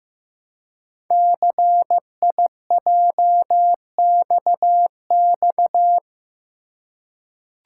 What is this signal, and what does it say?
Morse code, heard as a keyed tone, about 15 wpm, 700 Hz. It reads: CIJXX
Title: Code Groups - 5 Letters - 15wpm